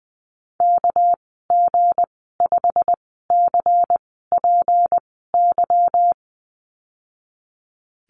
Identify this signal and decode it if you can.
Morse code, heard as a keyed tone, about 19 wpm, spaced as for 14 wpm, 700 Hz. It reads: KG5CPY
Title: 231 Lesson 37g - Letter x - Call Signs 20wpm